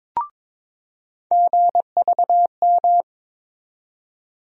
Morse code, heard as a keyed tone, about 22 wpm, 700 Hz. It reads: GVM
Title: USA Call Sign Postfixes - 3 Letters 22wpm